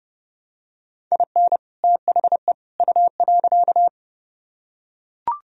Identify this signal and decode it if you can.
Morse code, heard as a keyed tone, about 28 wpm, 700 Hz. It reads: IN THE U.